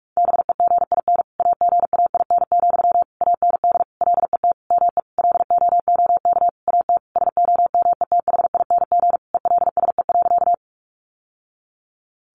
Morse code, heard as a keyed tone, about 44 wpm, 700 Hz. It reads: BEGIN AGAIN, AND LET ME LOOK AT SOMETHING ELSE.
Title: Sentences from Top 800 Words 45wpm